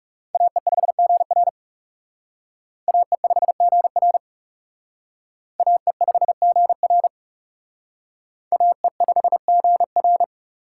Morse code, heard as a keyed tone, about 36 wpm, 700 Hz. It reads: AE5GR AE5GR AE5GR AE5GR
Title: Speed Racing - USA 2x2 Call Signs 30wpm